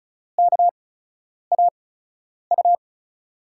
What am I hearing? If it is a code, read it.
Morse code, heard as a keyed tone, about 34 wpm, spaced as for 10 wpm, 700 Hz. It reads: KAU
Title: USA Call Signs - 8x Character Spacing - 35wpm